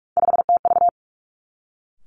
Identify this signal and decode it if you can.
Morse code, heard as a keyed tone, about 46 wpm, 700 Hz. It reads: 5TV
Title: Three Challenging Letter-Number Combinations 45wpm